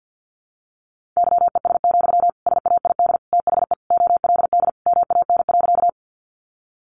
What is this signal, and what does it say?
Morse code, heard as a keyed tone, about 50 wpm, 700 Hz. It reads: YES, SAID THE OLD MAN.
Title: Sentences from Top 500 Words 50wpm